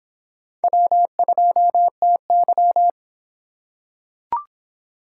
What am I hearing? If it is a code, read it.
Morse code, heard as a keyed tone, about 24 wpm, 700 Hz. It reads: W2TY